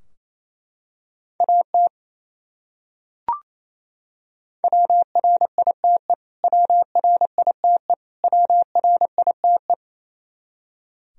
Morse code, heard as a keyed tone, about 28 wpm, 700 Hz. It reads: AT WRITE WRITE WRITE
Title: Top 100 Words Repeated 28wpm